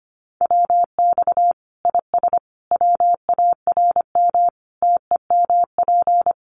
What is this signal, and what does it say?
Morse code, heard as a keyed tone, about 25 wpm, 700 Hz. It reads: WX IS WARM TEMP
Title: Common QSO Phrases 25wpm